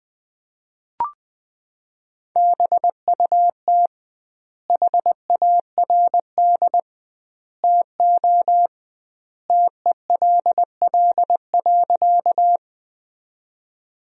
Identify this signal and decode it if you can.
Morse code, heard as a keyed tone, about 20 wpm, 700 Hz. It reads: BUT HARD TO TELL.
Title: 2x Word-Spacing - Sentences from Top 300 Words - 20wpm